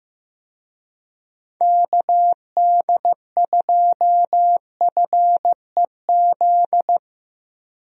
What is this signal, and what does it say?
Morse code, heard as a keyed tone, about 15 wpm, 700 Hz. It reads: KD2FEZ
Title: USA Call Signs - 6 Characters - Repeated 2x - 15wpm